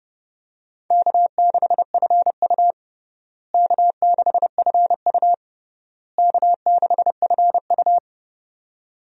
Morse code, heard as a keyed tone, about 32 wpm, 700 Hz. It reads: K6FU K6FU K6FU
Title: USA Call Signs - 4 Characters - Repeated 3x - 30wpm